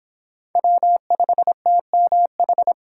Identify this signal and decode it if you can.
Morse code, heard as a keyed tone, about 31 wpm, 700 Hz. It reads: W5TMH